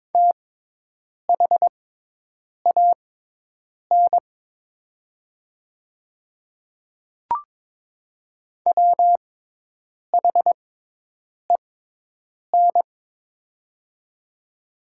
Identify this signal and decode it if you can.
Morse code, heard as a keyed tone, about 22 wpm, spaced as for 8 wpm, 700 Hz. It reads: THAN WHEN
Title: 6x Character Spacing - Top 100 Words - 22wpm